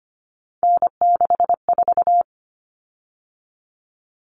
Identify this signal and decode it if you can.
Morse code, heard as a keyed tone, about 25 wpm, 700 Hz. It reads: N64
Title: Common Cut Numbers - 3-Digit 25wpm